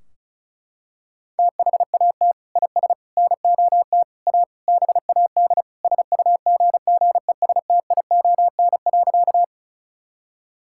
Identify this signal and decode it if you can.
Morse code, heard as a keyed tone, about 34 wpm, 700 Hz. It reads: THAT IS NOT A BAD SUGGESTION.
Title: Sentences from Top 1600 Words 35wpm